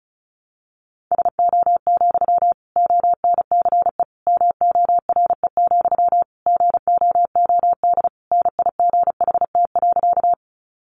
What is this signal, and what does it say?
Morse code, heard as a keyed tone, about 34 wpm, 700 Hz. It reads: SO, ONCE MORE, GOOD NIGHT.
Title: Sentences from Top 300 Words 35wpm